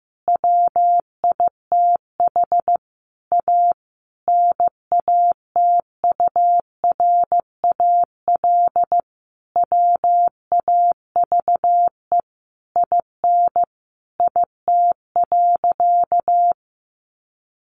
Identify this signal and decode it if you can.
Morse code, heard as a keyed tone, about 14 wpm, 700 Hz. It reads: WITH A NATURAL WAVE IN IT.
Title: Sentences from Top 1400 Words 15wpm